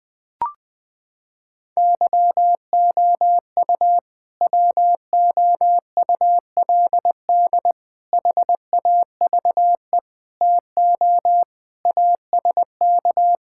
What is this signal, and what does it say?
Morse code, heard as a keyed tone, about 20 wpm, 700 Hz. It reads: YOU WOULD HAVE TO ASK
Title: Sentences with 3-Word Phrases - 20wpm